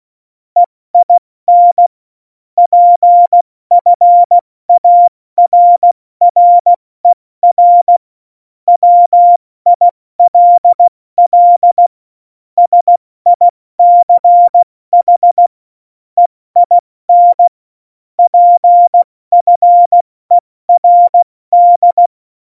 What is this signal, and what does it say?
Morse code, heard as a keyed tone, about 16 wpm, 700 Hz. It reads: EIN PFARRER WILL SICH EIN PFERD
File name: Morsekurs / Witze / Witz 26 Mai, 2020 70 F80